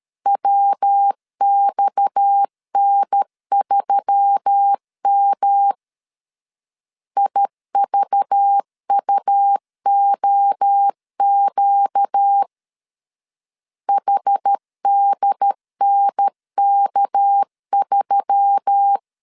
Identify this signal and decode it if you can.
Morse code, heard as a keyed tone, about 13 wpm, 800 Hz. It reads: WXN3M IVUOQ HDNK3
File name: CW / DET / 1 6 1500C S A 12wpm 3440